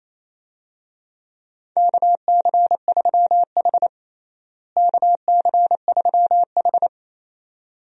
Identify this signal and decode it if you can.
Morse code, heard as a keyed tone, about 28 wpm, 700 Hz. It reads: KC3H KC3H
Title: USA Call Signs - 4 Characters - Repeated 2x - 28wpm